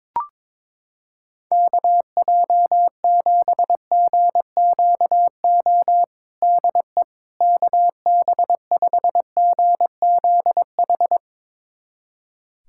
Morse code, heard as a keyed tone, about 21 wpm, 700 Hz. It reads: KJ7GQO DE KB5GZH
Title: USA Callsign DE Callsign - 6 Characters - 22wpm